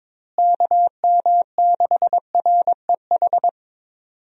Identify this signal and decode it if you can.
Morse code, heard as a keyed tone, about 22 wpm, 700 Hz. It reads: KM6REH
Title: USA Call Signs - 6 Characters - Repeated 2x - 22wpm